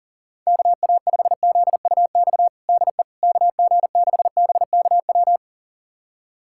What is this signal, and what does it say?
Morse code, heard as a keyed tone, about 37 wpm, 700 Hz. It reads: KA5ZUX DE KG6BKW
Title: USA Callsign DE Callsign 40wpm